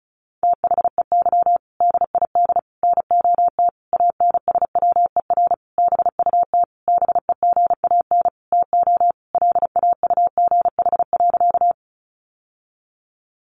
Morse code, heard as a keyed tone, about 34 wpm, 700 Hz. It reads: THEY DID NOT ANSWER BUT BEGAN TO LAUGH.
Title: Sentences from Top 600 Words 35wpm